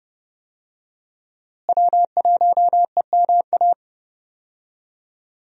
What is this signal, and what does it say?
Morse code, heard as a keyed tone, about 30 wpm, 700 Hz. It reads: W1EMA
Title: USA Call Signs - 5 Characters - Repeated 3x - 30wpm